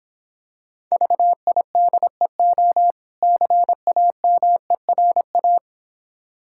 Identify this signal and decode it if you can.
Morse code, heard as a keyed tone, about 26 wpm, 690 Hz. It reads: VIDEO CAMERA